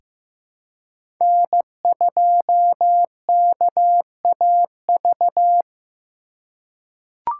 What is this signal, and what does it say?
Morse code, heard as a keyed tone, about 15 wpm, 700 Hz. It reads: N2KAV
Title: USA Call Signs - 5 Characters - 3x Character Spacing - 15wpm